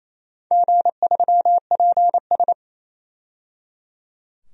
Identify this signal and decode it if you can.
Morse code, heard as a keyed tone, about 28 wpm, 700 Hz. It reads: G3PS
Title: DXCC-100 Call Signs - 4 Characters - 28wpm